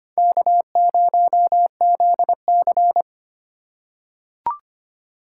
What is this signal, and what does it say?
Morse code, heard as a keyed tone, about 25 wpm, 700 Hz. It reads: K0ZC
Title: USA Call Signs - 4 Characters - 2x Character Spacing - 25wpm